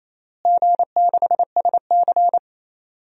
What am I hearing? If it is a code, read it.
Morse code, heard as a keyed tone, about 28 wpm, 700 Hz. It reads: G6SC
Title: DXCC-100 Call Signs - 4 Characters - Repeated 3x - 28wpm